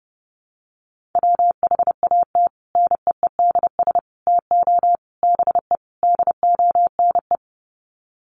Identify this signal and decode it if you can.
Morse code, heard as a keyed tone, about 30 wpm, 700 Hz. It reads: WHAT NEEDS TO BE DONE
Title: Sets of 5 Words Encore 30wpm